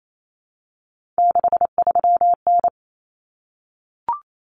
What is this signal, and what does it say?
Morse code, heard as a keyed tone, about 28 wpm, 700 Hz. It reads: 63N
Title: Common Cut Numbers - 3-Digit 28wpm